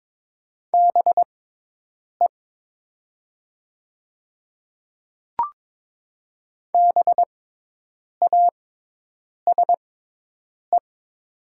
Morse code, heard as a keyed tone, about 22 wpm, spaced as for 8 wpm, 700 Hz. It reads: BE BASE
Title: 6x Character Spacing - Top 300 Words - 22wpm